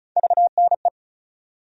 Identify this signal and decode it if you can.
Morse code, heard as a keyed tone, about 36 wpm, 700 Hz. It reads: VNE
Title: USA Call Sign Postfixes - 3 Letters 35wpm